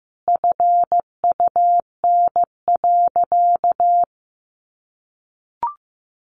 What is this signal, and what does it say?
Morse code, heard as a keyed tone, about 15 wpm, 700 Hz. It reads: FUN.
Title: Easy Phrases 15wpm